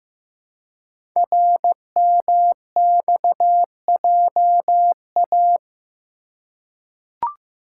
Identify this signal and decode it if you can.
Morse code, heard as a keyed tone, about 15 wpm, 700 Hz. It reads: RMXJA
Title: Code Groups - 5 Letters - 15wpm